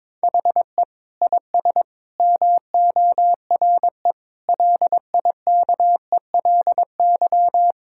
Morse code, heard as a keyed tone, about 21 wpm, 700 Hz. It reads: HE IS MORE LIKELY
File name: practice / 22wpm / Sentences with 4-Word Phrases - 22wpm